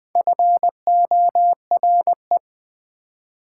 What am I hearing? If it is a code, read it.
Morse code, heard as a keyed tone, about 20 wpm, 700 Hz. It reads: FORE
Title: Prefixes and Suffixes 20wpm